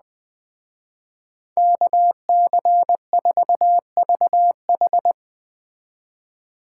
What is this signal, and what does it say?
Morse code, heard as a keyed tone, about 20 wpm, 700 Hz. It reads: KC4VH
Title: Speed Racing - USA 2x2 Call Signs 17wpm